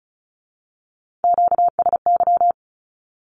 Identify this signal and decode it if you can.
Morse code, heard as a keyed tone, about 36 wpm, 700 Hz. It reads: QSY
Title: Q-Codes with Spoken Meaning - 35wpm